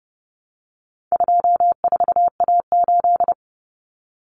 Morse code, heard as a keyed tone, about 30 wpm, 700 Hz. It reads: 24A8
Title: Common Cut Numbers - 4-Digit 30wpm